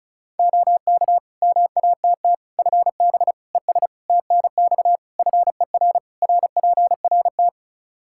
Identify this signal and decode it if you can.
Morse code, heard as a keyed tone, about 36 wpm, 700 Hz. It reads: OK MATT FB ES TNX FER RPRT
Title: Practice QSO - Exchange 2 - Info - 35wpm